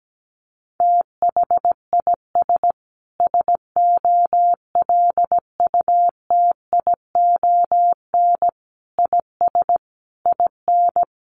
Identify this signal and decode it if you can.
Morse code, heard as a keyed tone, about 17 wpm, 700 Hz. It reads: THIS SOLUTION IS IN
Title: Sentences from Top 1300 Words 17wpm